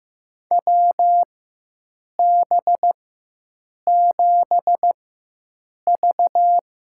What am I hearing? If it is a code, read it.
Morse code, heard as a keyed tone, about 15 wpm, spaced as for 7 wpm, 700 Hz. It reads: WB7V
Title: USA Call Signs - 4x Character Spacing - 15wpm